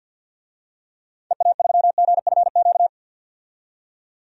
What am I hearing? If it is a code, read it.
Morse code, heard as a keyed tone, about 51 wpm, 700 Hz. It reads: EA3GFQ
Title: DXCC-100 Call Signs - 6 Characters - 50wpm